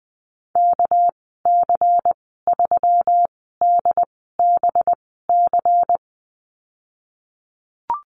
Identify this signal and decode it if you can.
Morse code, heard as a keyed tone, about 20 wpm, spaced as for 14 wpm, 700 Hz. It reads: KC3DBC
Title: 200 Lesson 32g - Letter k - Call Signs 20wpm